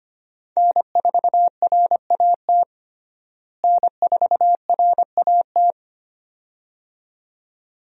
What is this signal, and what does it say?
Morse code, heard as a keyed tone, about 25 wpm, 700 Hz. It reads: N4RAT N4RAT
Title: USA Call Signs - 5 Characters - Repeated 2x - 25wpm